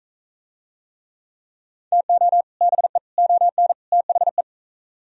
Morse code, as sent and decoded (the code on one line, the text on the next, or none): - --- / -... / --- -. / - .... .
TO BE ON THE